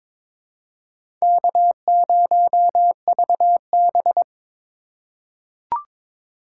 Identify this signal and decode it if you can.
Morse code, heard as a keyed tone, about 22 wpm, 700 Hz. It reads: K0VB
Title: USA Call Signs - 4x Character Spacing - 22wpm